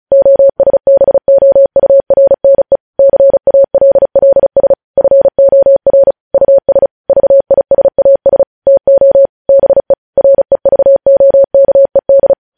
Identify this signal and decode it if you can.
Morse code, heard as a keyed tone, about 36 wpm, 550 Hz. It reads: OSBOURNE CALLS FOR US VISAS TO BE REVOKED
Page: news headlines in morse code at 35 wpm